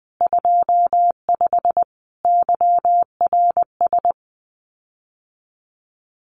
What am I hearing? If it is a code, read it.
Morse code, heard as a keyed tone, about 20 wpm, 700 Hz. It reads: 25 YRS
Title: Common QSO Phrases Encore 20wpm